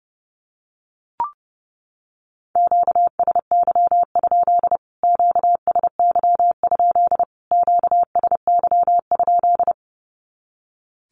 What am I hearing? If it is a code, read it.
Morse code, heard as a keyed tone, about 28 wpm, 700 Hz. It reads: QSY? QSY? QSY?
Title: Q-Codes Repeated 3x 30wpm